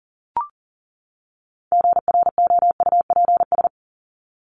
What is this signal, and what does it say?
Morse code, heard as a keyed tone, about 40 wpm, 700 Hz. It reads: GROUPS
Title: ES:S Suffix - Top 100 Words with Base Words 40wpm